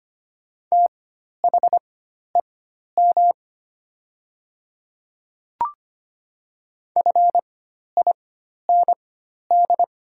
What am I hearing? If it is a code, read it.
Morse code, heard as a keyed tone, about 25 wpm, spaced as for 12 wpm, 700 Hz. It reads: THEM FIND
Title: 4x Character Spacing - Top 200 Words - 25wpm